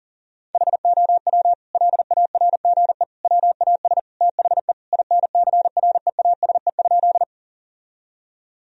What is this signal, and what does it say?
Morse code, heard as a keyed tone, about 40 wpm, 700 Hz. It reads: HOW LARGE WAS THE INCREASE?
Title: Sentences from Top 1000 Words - Review Part 4 - 40wpm